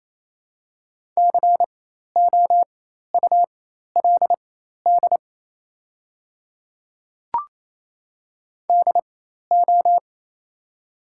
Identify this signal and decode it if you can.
Morse code, heard as a keyed tone, about 28 wpm, spaced as for 13 wpm, 700 Hz. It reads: COULD DO
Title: 4x Character Spacing - Top 300 Words - 28wpm